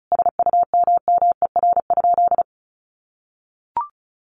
Morse code, heard as a keyed tone, about 34 wpm, 700 Hz. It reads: SUMMER?
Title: Sentences from Top 1200 Words 35wpm